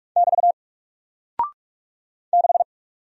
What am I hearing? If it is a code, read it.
Morse code, heard as a keyed tone, about 44 wpm, 700 Hz. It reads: <BT> 6
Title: Single Character (Letter:Number:Punctuation:Prosign) Alt 45wpm